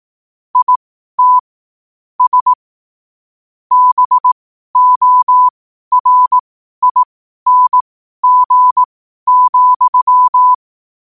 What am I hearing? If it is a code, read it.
Morse code, heard as a keyed tone, about 18 wpm, spaced as for 12 wpm, 1000 Hz. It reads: IT S BORING,
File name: qotdfiles / 12wpm